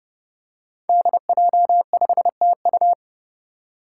Morse code, heard as a keyed tone, about 30 wpm, 700 Hz. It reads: DJ5TU
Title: Five Challenging Letter-Number Combinations 30wpm